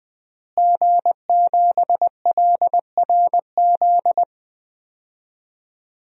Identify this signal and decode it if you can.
Morse code, heard as a keyed tone, about 20 wpm, 700 Hz. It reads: G7LRZ
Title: DXCC-100 Call Signs - 5 Characters - Repeated 2x - 20wpm